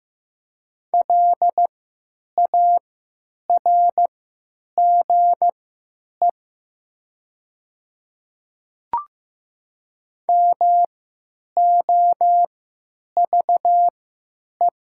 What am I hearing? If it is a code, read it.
Morse code, heard as a keyed tone, about 15 wpm, spaced as for 9 wpm, 700 Hz. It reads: LARGE MOVE
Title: 3x Character Spacing - Top 300 Words - 15wpm